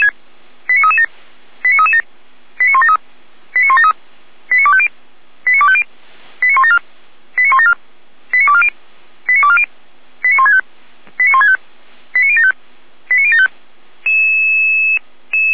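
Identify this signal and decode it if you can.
Five-tone selective call (ZVEI), heard as quick runs of five tones, tones of about 70 ms. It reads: 89337 89337 89173 89173 89250 89250 89174 89174 89240 89240 89165 89165 89075 89075
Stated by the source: ZVEI five-tone selective call 89160 received by SDR at feuerwehr-orlamuende.org fire brigade.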